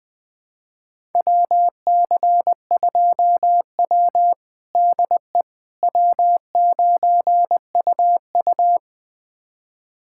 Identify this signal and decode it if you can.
Morse code, heard as a keyed tone, about 20 wpm, 700 Hz. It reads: WC2W DE W9UU
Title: USA Callsign DE Callsign - 4 Characters - 20wpm